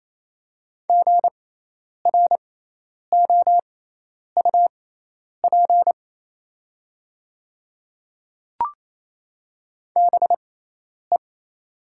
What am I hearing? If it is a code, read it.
Morse code, heard as a keyed tone, about 28 wpm, spaced as for 10 wpm, 700 Hz. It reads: GROUP BE
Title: 6x Character Spacing - Top 300 Words - 28wpm